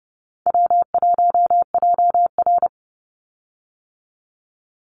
Morse code, heard as a keyed tone, about 30 wpm, 700 Hz. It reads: W1JR
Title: Call Signs - 30wpm